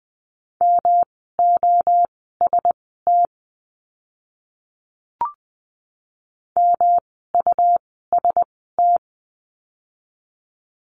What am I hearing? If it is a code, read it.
Morse code, heard as a keyed tone, about 20 wpm, spaced as for 14 wpm, 700 Hz. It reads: MOST MUST
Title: 237 Lesson 38f - Letter q - Words 20wpm